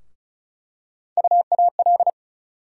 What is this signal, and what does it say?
Morse code, heard as a keyed tone, about 36 wpm, 700 Hz. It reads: UAL
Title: USA Call Sign Postfixes - 3 Letters 35wpm